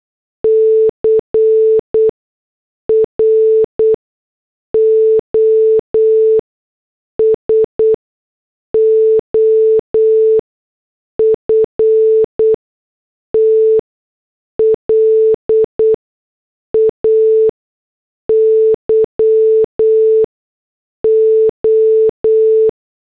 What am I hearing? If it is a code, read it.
Morse code, heard as a keyed tone, about 8 wpm, spaced as for 6 wpm, 440 Hz. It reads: CROSOFT LAYO
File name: morse / headlines